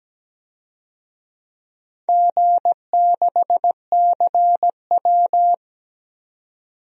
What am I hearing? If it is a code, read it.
Morse code, heard as a keyed tone, about 17 wpm, 700 Hz. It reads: G6CW
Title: DXCC-100 Call Signs - 4 Characters - 17wpm